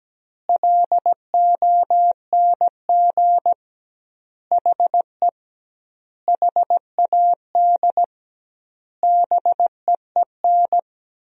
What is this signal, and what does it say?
Morse code, heard as a keyed tone, about 17 wpm, 700 Hz. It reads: LONG HE HAD BEEN
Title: 2x Word-Spacing - Sentences from Top 300 Words - 17wpm